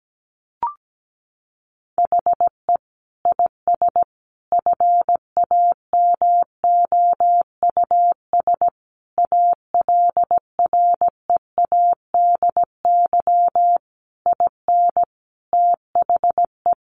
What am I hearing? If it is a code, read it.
Morse code, heard as a keyed tone, about 17 wpm, 700 Hz. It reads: HE IS FAMOUS ALREADY IN THE
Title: Sentences from Top 1500 Words 17wpm